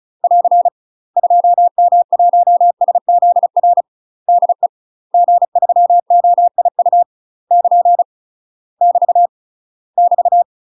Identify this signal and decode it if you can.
Morse code, heard as a keyed tone, about 35 wpm, 700 Hz. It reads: <AR> 2M1SZR DE G3OIU <KN> <BT> <BT>
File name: UK MASTER QSOs TRAINING / 35WPM / UK MASTER QSOs 0039